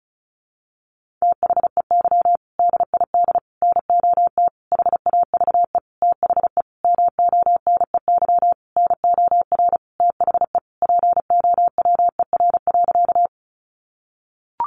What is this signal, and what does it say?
Morse code, heard as a keyed tone, about 34 wpm, 700 Hz. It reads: THEY DID NOT HAVE THE MONEY NOR THE POWER.
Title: Sentences from Top 1500 Words 35wpm